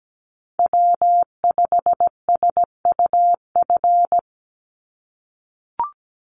Morse code, heard as a keyed tone, about 17 wpm, 700 Hz. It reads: W5SUF
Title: USA Call Signs 17wpm